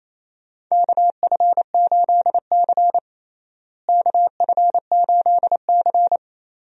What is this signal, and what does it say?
Morse code, heard as a keyed tone, about 28 wpm, 700 Hz. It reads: KF8C KF8C
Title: USA Call Signs - 4 Characters - Repeated 3x - 28wpm